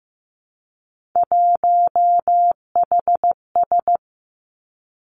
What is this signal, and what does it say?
Morse code, heard as a keyed tone, about 15 wpm, 700 Hz. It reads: 1HS
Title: Three Challenging Letter-Number Combinations 15wpm